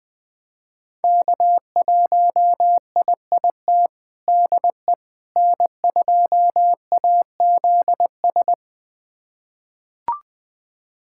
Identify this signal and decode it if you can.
Morse code, heard as a keyed tone, about 20 wpm, 700 Hz. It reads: K1IIT DE N2AZS
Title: USA Callsign DE Callsign - 5 Characters - 20wpm